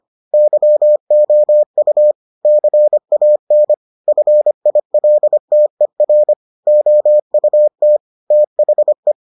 Morse code, heard as a keyed tone, about 25 wpm, 600 Hz. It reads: YOU CAN FILTER OUT THE